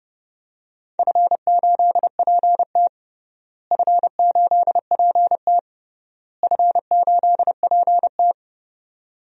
Morse code, heard as a keyed tone, about 30 wpm, 700 Hz. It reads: F8PT F8PT F8PT
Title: DXCC-100 Call Signs - 4 Characters - Repeated 3x - 30wpm